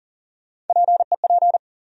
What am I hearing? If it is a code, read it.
Morse code, heard as a keyed tone, about 40 wpm, 700 Hz. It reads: PEP
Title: USA Call Sign Postfixes - 3 Letters 40wpm